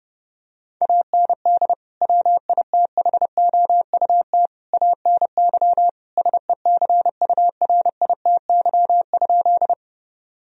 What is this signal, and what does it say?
Morse code, heard as a keyed tone, about 30 wpm, 700 Hz. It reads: AND WITHOUT ANY SECURITY?
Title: Sentences from Top 1000 Words - Review Part 4 - 30wpm